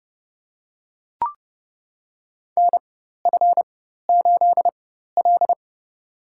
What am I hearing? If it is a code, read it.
Morse code, heard as a keyed tone, about 30 wpm, spaced as for 14 wpm, 700 Hz. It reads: NF8L
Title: USA Call Signs - 4 Characters - 4x Character Spacing - 30wpm